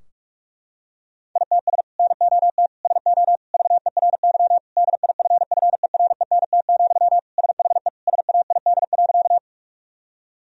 Morse code, heard as a keyed tone, about 46 wpm, 700 Hz. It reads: ITS NOT SO VERY DIFFERENT, SHE SAID.
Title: Sentences from Top 1000 Words - Review Part 3 - 45wpm